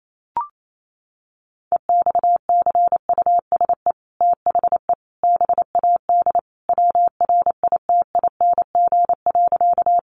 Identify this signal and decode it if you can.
Morse code, heard as a keyed tone, about 28 wpm, 700 Hz. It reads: EXCUSE THE BAD WRITING.
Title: Sentences from Top 800 Words 28wpm